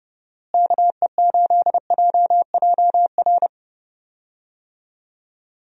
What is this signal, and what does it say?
Morse code, heard as a keyed tone, about 30 wpm, 700 Hz. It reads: KE8JJR
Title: USA Call Signs - 6 Characters - Repeated 3x - 30wpm